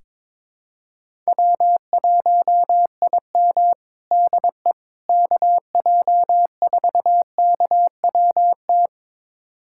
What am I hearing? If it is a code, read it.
Morse code, heard as a keyed tone, about 22 wpm, 700 Hz. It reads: W1IM DE KJ4KWT
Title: USA Callsign DE Callsign 22wpm